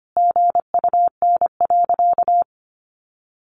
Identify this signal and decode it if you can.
Morse code, heard as a keyed tone, about 25 wpm, 700 Hz. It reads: GUN.
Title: A Sound Of Thunder 25wpm